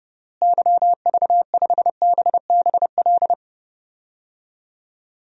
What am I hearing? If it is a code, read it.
Morse code, heard as a keyed tone, about 30 wpm, 700 Hz. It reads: YV5BBL
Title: DXCC-100 Call Signs - 6 Characters - Repeated 3x - 30wpm